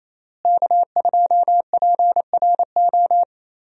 Morse code, heard as a keyed tone, about 28 wpm, 700 Hz. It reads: K2PRO
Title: USA Call Signs - 5 Characters - Repeated 3x - 28wpm